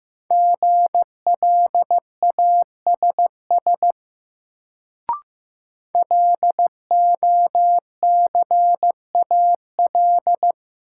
Morse code, heard as a keyed tone, about 16 wpm, 700 Hz. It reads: GLASS LOCAL
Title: Sentences from Top 1000 Words - Review Part 3 - New Word Drill - 15wpm